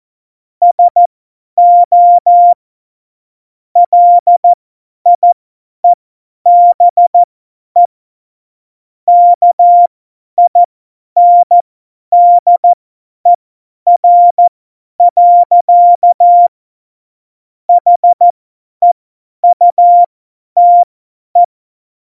Morse code, heard as a keyed tone, about 14 wpm, spaced as for 10 wpm, 700 Hz. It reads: SO LIEBE KINDER. HEUTE
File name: Morsekurs / Witze / Witz 23. April 2019, 50 F70